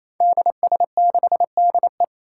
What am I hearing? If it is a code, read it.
Morse code, heard as a keyed tone, about 29 wpm, 700 Hz. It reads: DS6DE
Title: Five Challenging Letter-Number Combinations 28wpm